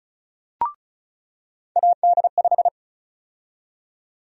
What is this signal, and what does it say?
Morse code, heard as a keyed tone, about 36 wpm, 700 Hz. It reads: AD5
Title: USA Call Sign Prefixes - 35wpm